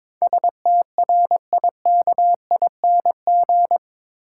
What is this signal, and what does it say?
Morse code, heard as a keyed tone, about 21 wpm, 700 Hz. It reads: STRIKING
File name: practice / 22wpm / Eight Letter Words - 22wpm